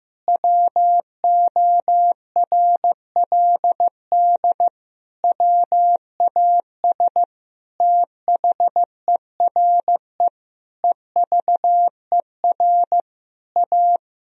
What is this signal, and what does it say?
Morse code, heard as a keyed tone, about 15 wpm, 700 Hz. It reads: WORLD WAS THERE EVER A
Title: Sentences from Top 1000 Words - Review Part 4 - 15wpm